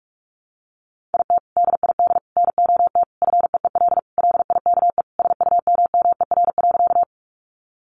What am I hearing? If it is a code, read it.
Morse code, heard as a keyed tone, about 44 wpm, 700 Hz. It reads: IT DID NOT FEEL LIKE SUMMER.
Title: Sentences from Top 1100 Words 45wpm